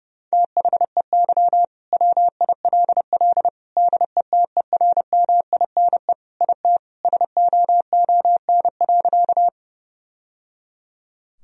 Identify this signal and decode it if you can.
Morse code, heard as a keyed tone, about 30 wpm, 700 Hz. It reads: THEY WILL DETERMINE IT SOON.
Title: Sentences from Top 1000 Words - Review Part 4 - 30wpm